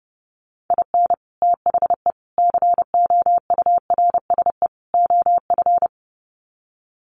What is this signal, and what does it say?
Morse code, heard as a keyed tone, about 30 wpm, 700 Hz. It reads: IN THE COURSE OF
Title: Sets of 4 Words 30wpm